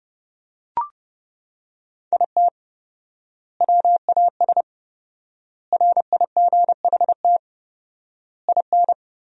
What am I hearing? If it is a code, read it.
Morse code, heard as a keyed tone, about 29 wpm, 700 Hz. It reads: IT WAS RIGHT IN
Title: 4x Word-Spacing - Sentences from Top 500 Words - 30wpm